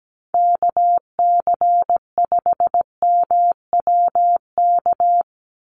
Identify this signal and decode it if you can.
Morse code, heard as a keyed tone, about 17 wpm, 700 Hz. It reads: KC5MWK
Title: USA Call Signs 17wpm